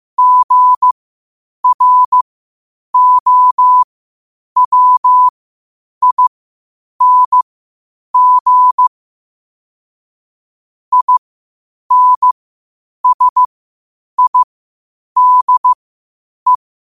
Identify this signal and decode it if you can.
Morse code, heard as a keyed tone, about 15 wpm, spaced as for 8 wpm, 1000 Hz. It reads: GROWING INSIDE
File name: qotdfiles / 8wpm